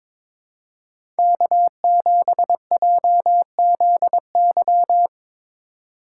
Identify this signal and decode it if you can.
Morse code, heard as a keyed tone, about 22 wpm, 700 Hz. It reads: K7JZY
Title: USA Call Signs - 5 Characters - 3x Character Spacing - 22wpm